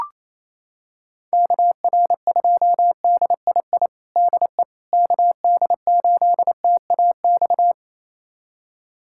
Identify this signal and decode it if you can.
Morse code, heard as a keyed tone, about 28 wpm, 700 Hz. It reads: KR2DII DE KD8TAX
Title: USA Callsign DE Callsign 28wpm